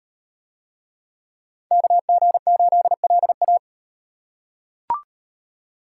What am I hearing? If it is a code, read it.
Morse code, heard as a keyed tone, about 36 wpm, 700 Hz. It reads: KG8LA